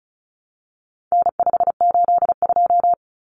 Morse code, heard as a keyed tone, about 34 wpm, 700 Hz. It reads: N582
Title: Common Cut Numbers - 4-Digit 35wpm